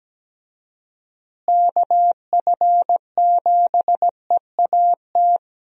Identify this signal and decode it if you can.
Morse code, heard as a keyed tone, about 17 wpm, 700 Hz. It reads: KF7EAT
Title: USA Call Signs - 6 Characters - Repeated 3x - 17wpm